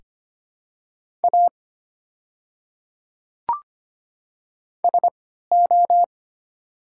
Morse code, heard as a keyed tone, about 25 wpm, spaced as for 14 wpm, 700 Hz. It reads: A SO
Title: 3x Character Spacing - Top 200 Words - 25wpm